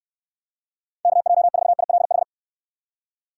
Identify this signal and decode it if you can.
Morse code, heard as a keyed tone, about 69 wpm, 700 Hz. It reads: KJ4ILH